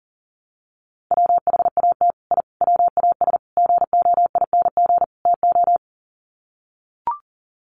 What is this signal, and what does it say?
Morse code, heard as a keyed tone, about 40 wpm, 700 Hz. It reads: WHAT I WAS GOING TO
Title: Sets of 5 Words Encore2 40wpm